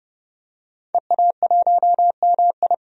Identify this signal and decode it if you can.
Morse code, heard as a keyed tone, about 30 wpm, 700 Hz. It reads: EA1MI
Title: DXCC-100 Call Signs - 5 Characters - 30wpm